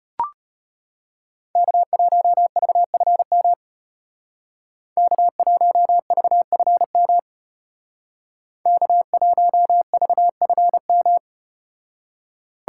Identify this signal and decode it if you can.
Morse code, heard as a keyed tone, about 33 wpm, 700 Hz. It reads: K1VFM K1VFM K1VFM